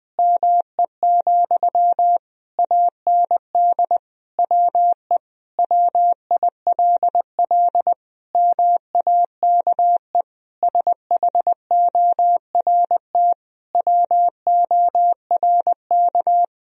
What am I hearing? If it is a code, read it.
Morse code, heard as a keyed tone, about 20 wpm, 700 Hz. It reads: ME, AND WE WILL MAKE SHORT WORK
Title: Sentences from Top 1000 Words - Review Part 2 - 20wpm